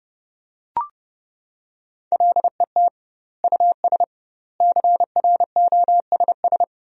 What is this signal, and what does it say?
Morse code, heard as a keyed tone, about 30 wpm, 700 Hz. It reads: LET US CROSS
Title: 2x Word-Spacing - Sentences from Top 300 Words - 30wpm